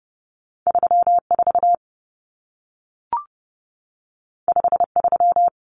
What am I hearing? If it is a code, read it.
Morse code, heard as a keyed tone, about 30 wpm, 700 Hz. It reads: 34 53
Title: Numbers - 2-Digit 30wpm